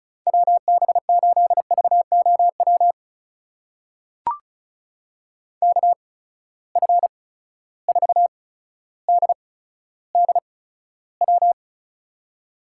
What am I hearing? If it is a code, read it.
Morse code, heard as a keyed tone, about 36 wpm, spaced as for 15 wpm, 700 Hz. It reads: WB8VOW KF4DDW
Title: USA Call Signs - 6 Characters - 8x Character Spacing - 35wpm